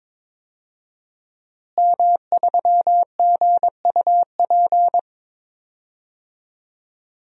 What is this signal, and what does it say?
Morse code, heard as a keyed tone, about 21 wpm, 700 Hz. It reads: M3GUP
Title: DXCC-100 Call Signs - 5 Characters - Repeated 2x - 22wpm